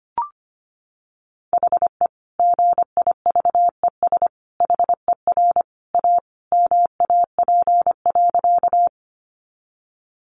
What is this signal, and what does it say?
Morse code, heard as a keyed tone, about 25 wpm, 700 Hz. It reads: HE GIVES HER A MAP.
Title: Additional Easy Phrases 25wpm